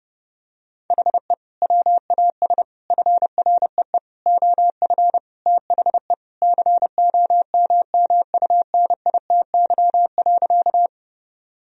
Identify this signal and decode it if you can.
Morse code, heard as a keyed tone, about 30 wpm, 700 Hz. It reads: HE WAS FREE OF THE COMMUNITY.
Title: Sentences from Top 1000 Words - Review Part 4 - 30wpm